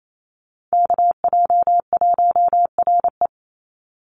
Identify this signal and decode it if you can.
Morse code, heard as a keyed tone, about 28 wpm, 700 Hz. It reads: KJ1RE
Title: Call Signs - 28wpm